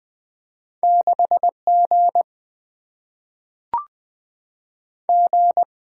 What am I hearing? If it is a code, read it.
Morse code, heard as a keyed tone, about 20 wpm, 700 Hz. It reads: 6 G G
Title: Single-Letter Single-Number 20wpm